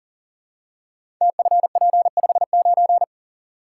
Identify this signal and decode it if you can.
Morse code, heard as a keyed tone, about 39 wpm, 700 Hz. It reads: TFP59
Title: Code Groups - 5 Letter:Numbers - 40wpm